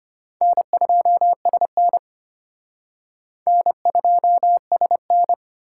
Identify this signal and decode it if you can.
Morse code, heard as a keyed tone, about 31 wpm, 700 Hz. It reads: N2SN N2SN
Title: Speed Racing - USA 1x2 Call Signs 25wpm